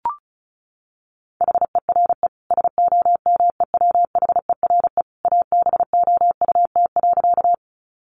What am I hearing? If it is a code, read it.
Morse code, heard as a keyed tone, about 34 wpm, 700 Hz. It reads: HERE SOMEWHERE ABOUT.
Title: Sentences from Top 1400 Words 35wpm